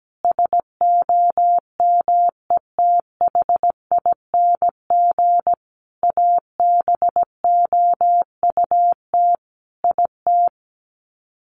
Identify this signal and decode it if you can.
Morse code, heard as a keyed tone, about 17 wpm, 700 Hz. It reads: SOMETHING ABOUT IT
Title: Sets of 5 Words 17wpm